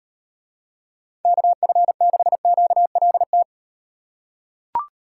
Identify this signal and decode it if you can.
Morse code, heard as a keyed tone, about 38 wpm, 700 Hz. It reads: KF6QLT